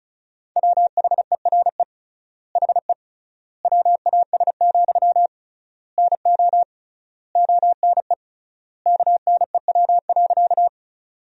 Morse code, heard as a keyed tone, about 34 wpm, 700 Hz. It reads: WHERE HE WAS, NO ONE KNEW.
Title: 3x Word-Spacing - Sentences from Top 400 Words - 35wpm